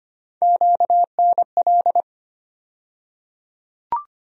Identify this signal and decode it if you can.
Morse code, heard as a keyed tone, about 25 wpm, 700 Hz. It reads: QNL
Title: USA Call Sign Postfixes - 3 Letters 25wpm